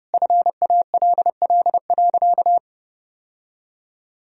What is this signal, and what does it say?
Morse code, heard as a keyed tone, about 32 wpm, 700 Hz. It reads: FALL.
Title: Sentences from Top 1000 Words - Review Part 2 - 30wpm